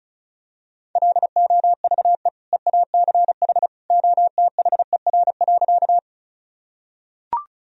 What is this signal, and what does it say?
Morse code, heard as a keyed tone, about 34 wpm, 700 Hz. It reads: LOVE EACH OTHER.
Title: Sentences from Top 1000 Words - Review Part 1 - 35wpm